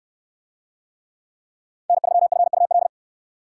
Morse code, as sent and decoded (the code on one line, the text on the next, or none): -. ...-- ..-. ..- .-..
N3FUL